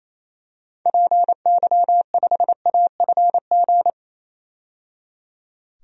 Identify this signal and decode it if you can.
Morse code, heard as a keyed tone, about 28 wpm, 700 Hz. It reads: PY5AFG
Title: DXCC-100 Call Signs - 6 Characters - 28wpm